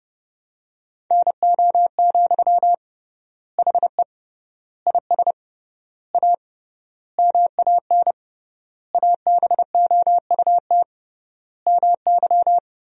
Morse code, heard as a keyed tone, about 30 wpm, 700 Hz. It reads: NO, HE IS A MAN ABOUT MY
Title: 3x Word-Spacing - Sentences from Top 400 Words - 30wpm